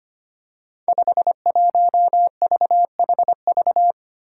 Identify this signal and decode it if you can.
Morse code, heard as a keyed tone, about 25 wpm, 700 Hz. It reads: 51VHV
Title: Five Challenging Letter-Number Combinations 25wpm